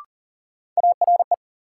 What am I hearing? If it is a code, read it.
Morse code, heard as a keyed tone, about 40 wpm, 700 Hz. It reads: ARE
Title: Sets of 2 Words - New Word Drill - 40wpm